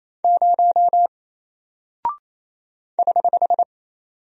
Morse code, heard as a keyed tone, about 27 wpm, 700 Hz. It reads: 0 <HH>
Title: Single Character (Letter:Number:Punctuation:Prosign) 28wpm